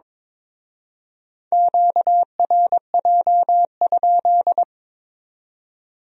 Morse code, heard as a keyed tone, about 26 wpm, 700 Hz. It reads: QRJ?
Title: Speed-Racing Q-Codes - 22wpm